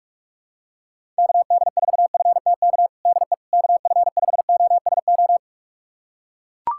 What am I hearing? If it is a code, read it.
Morse code, heard as a keyed tone, about 42 wpm, 700 Hz. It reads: KD4FTK DE KF5OSO
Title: USA Callsign DE Callsign 45wpm